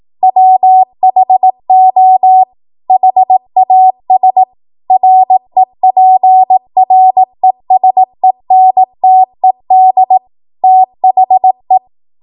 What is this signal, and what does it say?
Morse code, heard as a keyed tone, about 18 wpm, 750 Hz. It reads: WHO HAS REPRESENTED THE
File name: Archive / 18 WPM / 18WPM